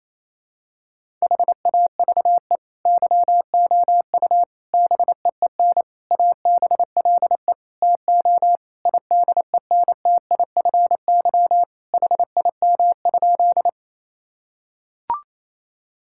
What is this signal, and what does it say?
Morse code, heard as a keyed tone, about 28 wpm, 700 Hz. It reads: HAVE YOU BEEN ABLE TO IDENTIFY HIM?
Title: Sentences from Top 1000 Words - Review Part 6 - 28wpm